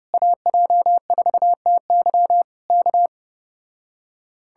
Morse code, heard as a keyed tone, about 29 wpm, 700 Hz. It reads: AJ4TY K